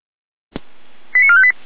Five-tone selective call (ZVEI), heard as quick runs of five tones, tones of about 80 ms. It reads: 89458